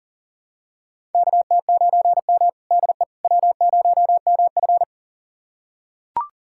Morse code, heard as a keyed tone, about 38 wpm, 700 Hz. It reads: KT9M DE W0MF
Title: USA Callsign DE Callsign - 4 Characters - 40wpm